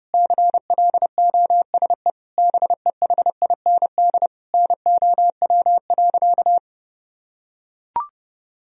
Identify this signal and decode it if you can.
Morse code, heard as a keyed tone, about 30 wpm, 700 Hz. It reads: CLOSE BEHIND NOW.
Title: Sentences from Top 1000 Words - Review Part 1 - 30wpm